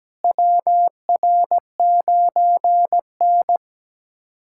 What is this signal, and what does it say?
Morse code, heard as a keyed tone, about 17 wpm, 700 Hz. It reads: WR9N
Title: USA Callsign DE Callsign - 4 Characters - 17wpm